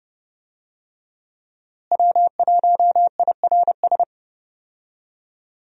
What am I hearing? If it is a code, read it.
Morse code, heard as a keyed tone, about 30 wpm, 700 Hz. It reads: W1IRS
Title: USA Call Signs - 5 Characters - Repeated 2x - 30wpm